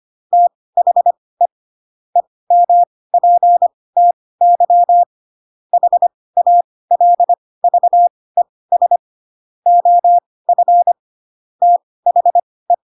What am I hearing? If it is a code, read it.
Morse code, heard as a keyed tone, about 25 wpm, spaced as for 18 wpm, 700 Hz. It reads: THE EMPTY HALVES OF THE